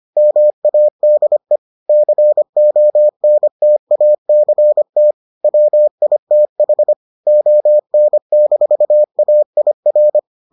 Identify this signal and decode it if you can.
Morse code, heard as a keyed tone, about 25 wpm, 600 Hz. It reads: MADE CONTACT WITH ON-AIR